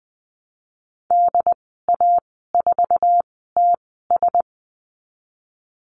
Morse code, heard as a keyed tone, about 20 wpm, spaced as for 14 wpm, 700 Hz. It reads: DA4TS
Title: 069 Lesson 12g - Letter d - Call Signs 20wpm